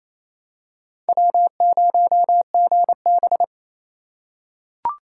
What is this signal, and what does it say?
Morse code, heard as a keyed tone, about 28 wpm, 700 Hz. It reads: W0GB
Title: USA Call Signs - 4x Character Spacing - 28wpm